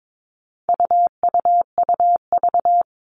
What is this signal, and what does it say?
Morse code, heard as a keyed tone, about 22 wpm, 700 Hz. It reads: UUUV
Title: All Cut Numbers - 4-Digit 22wpm